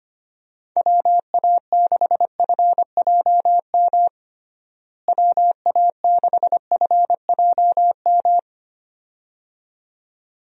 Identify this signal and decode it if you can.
Morse code, heard as a keyed tone, about 25 wpm, 700 Hz. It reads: WA6FJM WA6FJM
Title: USA Call Signs - 6 Characters - Repeated 2x - 25wpm